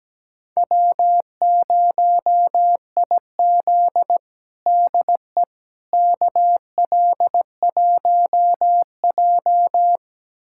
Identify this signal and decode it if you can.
Morse code, heard as a keyed tone, about 17 wpm, 700 Hz. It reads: W0IZ DE KL1J
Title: USA Callsign DE Callsign - 4 Characters - 17wpm